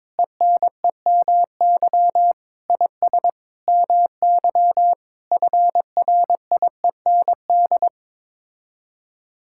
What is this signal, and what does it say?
Morse code, heard as a keyed tone, about 22 wpm, 700 Hz. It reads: ENEMY IS MY FRIEND
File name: practice / 22wpm / Sets of 8 Words - 22wpm